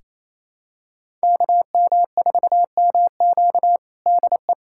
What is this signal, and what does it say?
Morse code, heard as a keyed tone, about 28 wpm, 700 Hz. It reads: KM4MQ DE
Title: USA Callsign DE Callsign - 5 Characters - 28wpm